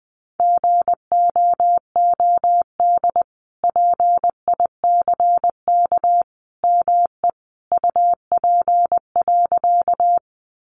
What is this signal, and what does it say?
Morse code, heard as a keyed tone, about 20 wpm, 700 Hz. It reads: GOOD PICK ME UP.
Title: Sentences from Top 1000 Words 20wpm